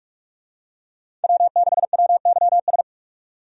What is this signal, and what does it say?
Morse code, heard as a keyed tone, about 47 wpm, 700 Hz. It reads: W6WYS
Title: USA Call Signs - 5 Characters - 45wpm